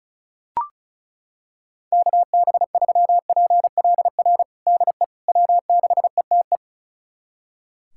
Morse code, heard as a keyed tone, about 36 wpm, 700 Hz. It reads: KB3PLR DE W6ETE
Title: USA Callsign DE Callsign 35wpm